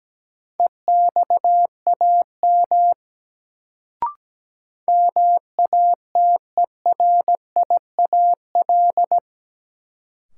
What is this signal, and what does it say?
Morse code, heard as a keyed tone, about 17 wpm, 700 Hz. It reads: EXAM MATERIAL
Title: Sentences from Top 1000 Words - Review Part 5 - New Word Drill - 17wpm